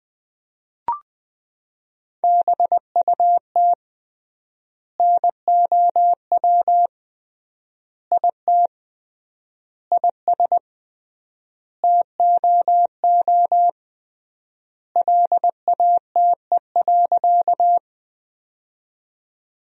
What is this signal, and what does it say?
Morse code, heard as a keyed tone, about 21 wpm, 700 Hz. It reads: BUT NOW IT IS TOO LATE.
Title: 3x Word-Spacing - Sentences from Top 300 Words - 20wpm